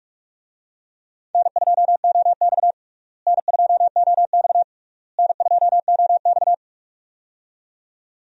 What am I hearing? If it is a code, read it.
Morse code, heard as a keyed tone, about 46 wpm, 700 Hz. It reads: N2OX N2OX N2OX
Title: USA Call Signs - 4 Characters - Repeated 3x - 45wpm